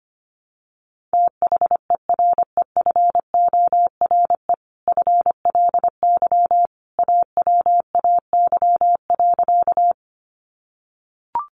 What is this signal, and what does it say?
Morse code, heard as a keyed tone, about 25 wpm, 700 Hz. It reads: THEREFORE FLY AWAY.
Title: Sentences from Top 800 Words 25wpm